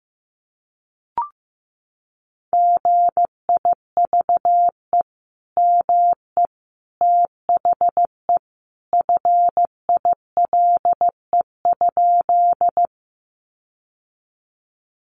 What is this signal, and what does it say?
Morse code, heard as a keyed tone, about 15 wpm, 700 Hz. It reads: GIVE ME THE FILE?
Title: Sentences from Top 1200 Words 15wpm